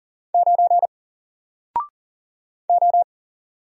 Morse code, heard as a keyed tone, about 38 wpm, 700 Hz. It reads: 9 O
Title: Single Letter-Number 40wpm